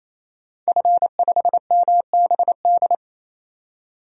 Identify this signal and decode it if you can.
Morse code, heard as a keyed tone, about 28 wpm, 700 Hz. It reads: F5MBD
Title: DXCC-100 Call Signs - 5 Characters - 28wpm